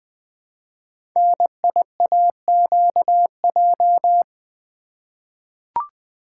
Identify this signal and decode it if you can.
Morse code, heard as a keyed tone, about 20 wpm, 700 Hz. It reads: NIAQJ
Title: Code Groups - 5 Letters - 20wpm